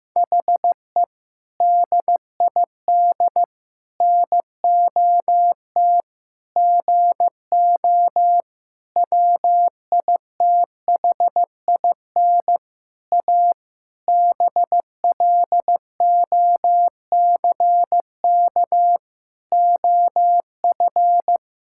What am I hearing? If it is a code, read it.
Morse code, heard as a keyed tone, about 15 wpm, 700 Hz. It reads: HE DID NOT GO WITHIN A BLOCK OF
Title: Sentences from Top 1600 Words 15wpm